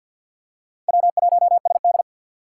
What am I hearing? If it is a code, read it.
Morse code, heard as a keyed tone, about 52 wpm, 700 Hz. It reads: W1SD
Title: USA Call Signs - 2x Character Spacing - 50wpm